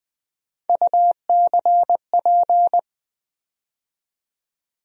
Morse code, heard as a keyed tone, about 21 wpm, 700 Hz. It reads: UCP
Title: USA Call Sign Postfixes - 3 Letters 20wpm